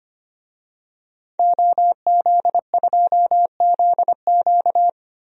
Morse code, heard as a keyed tone, about 25 wpm, 700 Hz. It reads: OZ2ZQ
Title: DXCC-100 Call Signs - 5 Characters - 25wpm